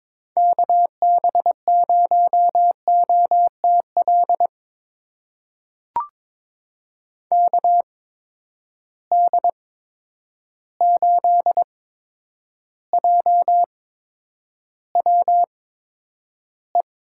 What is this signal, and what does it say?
Morse code, heard as a keyed tone, about 22 wpm, spaced as for 9 wpm, 700 Hz. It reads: KB0OTL KD8JWE
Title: USA Call Signs - 6 Characters - 8x Character Spacing - 22wpm